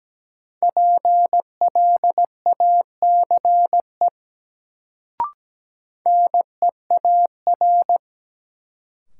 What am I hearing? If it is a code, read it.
Morse code, heard as a keyed tone, about 17 wpm, 700 Hz. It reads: PLACE NEAR
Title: Sentences from Top 200 Words - New Word Drill - 17wpm